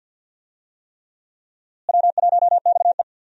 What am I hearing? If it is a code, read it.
Morse code, heard as a keyed tone, about 53 wpm, 700 Hz. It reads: W1XE
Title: USA Call Signs - 4 Characters - Repeated 3x - 50wpm